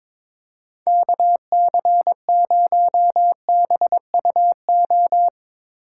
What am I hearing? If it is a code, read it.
Morse code, heard as a keyed tone, about 25 wpm, 700 Hz. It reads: KC0BUO
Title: Speed Racing - USA 2x3 Call Signs 22wpm